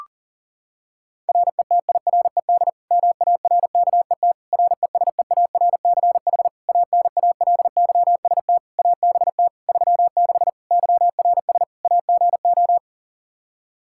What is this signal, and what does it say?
Morse code, heard as a keyed tone, about 40 wpm, 700 Hz. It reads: RETIRED MARKET RESEARCH ANALYST ABT 36 YRS AGO
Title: Practice QSO - Exchange 3 - More Info - 40wpm